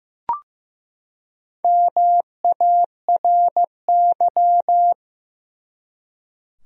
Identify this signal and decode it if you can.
Morse code, heard as a keyed tone, about 16 wpm, 700 Hz. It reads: MARY
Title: USA Common Names - 15wpm